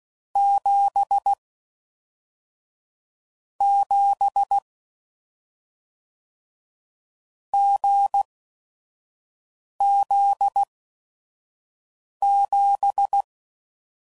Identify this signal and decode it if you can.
Morse code, heard as a keyed tone, about 16 wpm, spaced as for 5 wpm, 800 Hz. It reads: 7 7 GZ7
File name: Mixed / Short Runs / awj1ndb6gz7